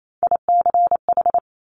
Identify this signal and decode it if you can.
Morse code, heard as a keyed tone, about 28 wpm, 690 Hz. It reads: ICH
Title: Top Three Letter Combinations in Words 28wpm